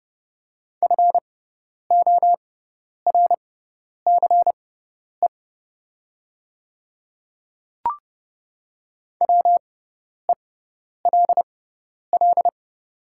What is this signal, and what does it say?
Morse code, heard as a keyed tone, about 30 wpm, spaced as for 10 wpm, 700 Hz. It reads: FORCE WELL
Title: 6x Character Spacing - Top 500 Words - 30wpm